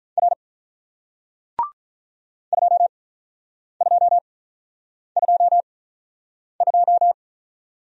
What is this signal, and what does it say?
Morse code, heard as a keyed tone, about 46 wpm, spaced as for 9 wpm, 700 Hz. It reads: R 2222